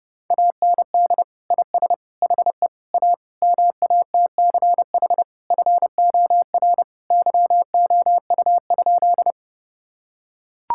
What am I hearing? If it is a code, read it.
Morse code, heard as a keyed tone, about 30 wpm, 700 Hz. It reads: AND IS HE A MATCH FOR YOU?
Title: Sentences from Top 1600 Words 30wpm